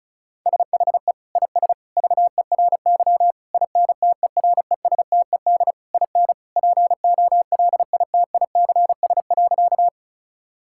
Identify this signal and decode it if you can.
Morse code, heard as a keyed tone, about 34 wpm, 700 Hz. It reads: SHE IS VERY INTERESTED IN POLITICS.
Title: Sentences from Top 1000 Words - Review Part 6 - 35wpm